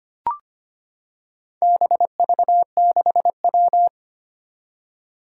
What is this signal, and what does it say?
Morse code, heard as a keyed tone, about 25 wpm, 700 Hz. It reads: BV6W
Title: Four Challenging Letter-Number Combinations 25wpm